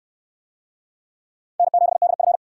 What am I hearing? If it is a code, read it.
Morse code, heard as a keyed tone, about 67 wpm, 700 Hz. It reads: N7DF